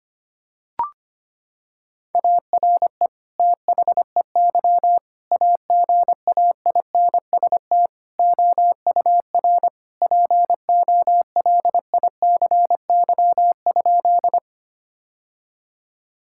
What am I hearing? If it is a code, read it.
Morse code, heard as a keyed tone, about 25 wpm, 700 Hz. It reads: ARE THEY AGAINST OUR POLICY?
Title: Sentences from Top 1000 Words - Review Part 5 - 25wpm